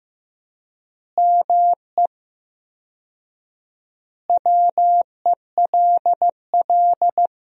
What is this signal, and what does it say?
Morse code, heard as a keyed tone, about 15 wpm, 700 Hz. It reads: ME WELL
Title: 4x Word-Spacing - Sentences from Top 400 Words - 15wpm